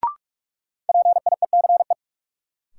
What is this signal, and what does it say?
Morse code, heard as a keyed tone, about 43 wpm, 700 Hz. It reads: PIECE
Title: Sets of 3 Words - New Word Drill - 45wpm